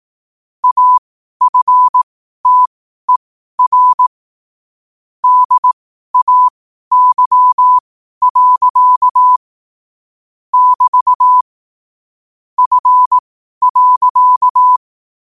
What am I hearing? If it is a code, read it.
Morse code, heard as a keyed tone, about 18 wpm, spaced as for 12 wpm, 1000 Hz. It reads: AFTER DAY. <BT> F.
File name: qotdfiles / 12wpm